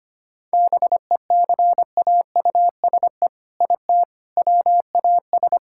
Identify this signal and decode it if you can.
Morse code, heard as a keyed tone, about 25 wpm, 700 Hz. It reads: BECAUSE IT WAS
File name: practice / 25wpm / All-Age Appropriate Jokes - 25wpm